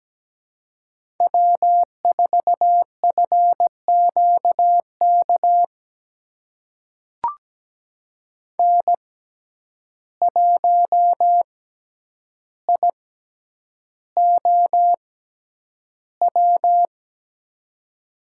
Morse code, heard as a keyed tone, about 17 wpm, spaced as for 9 wpm, 700 Hz. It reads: W4FQK N1IOW
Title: USA Call Signs - 6x Character Spacing - 17wpm